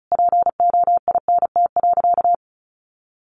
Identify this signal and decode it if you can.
Morse code, heard as a keyed tone, about 34 wpm, 700 Hz. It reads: POINT.
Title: Sentences from Top 600 Words 35wpm